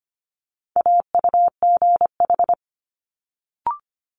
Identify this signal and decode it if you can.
Morse code, heard as a keyed tone, about 25 wpm, 700 Hz. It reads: AUGH
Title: Multi-letter Phonemes Quick Fire 25wpm